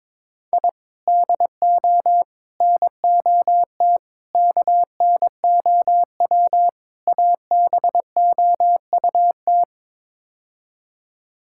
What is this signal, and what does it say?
Morse code, heard as a keyed tone, about 22 wpm, 700 Hz. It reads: I DO NOT KNOW ABOUT
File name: practice / 22wpm / Build A Phrase - 5-Word Phrases - 22wpm